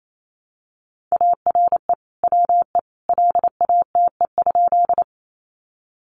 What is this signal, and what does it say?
Morse code, heard as a keyed tone, about 28 wpm, 700 Hz. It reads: ARE WE LATE?
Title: Sentences from Top 300 Words 28wpm